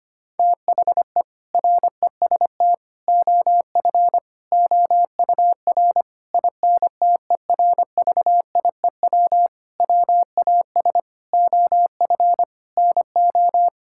Morse code, heard as a keyed tone, about 25 wpm, 700 Hz. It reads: THE REST OF OUR INTERVIEW WAS OF NO
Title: Sentences from Top 1600 Words 25wpm